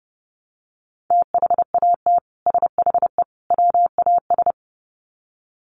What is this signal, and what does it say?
Morse code, heard as a keyed tone, about 30 wpm, 700 Hz. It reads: THAT SHE WAS
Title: Sets of 3 Words 30wpm